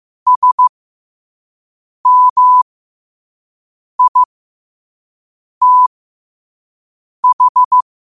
Morse code, heard as a keyed tone, about 15 wpm, spaced as for 5 wpm, 1000 Hz. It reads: SMITH
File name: qotdfiles / 5wpm